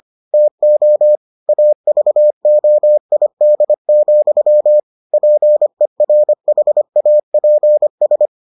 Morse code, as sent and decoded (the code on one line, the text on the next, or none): - --- / .- ...- --- .. -.. --..-- / .--. .-. .... .- .--. ...
TO AVOID, PERHAPS